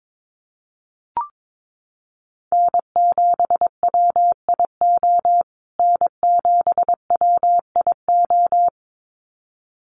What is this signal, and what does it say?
Morse code, heard as a keyed tone, about 22 wpm, 700 Hz. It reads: N7WIO N7WIO
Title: USA Call Signs Repeated 2x 22wpm